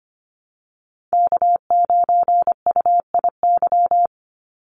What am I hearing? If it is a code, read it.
Morse code, heard as a keyed tone, about 25 wpm, 700 Hz. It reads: K9UIY
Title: Call Signs Repeated 3x - 25wpm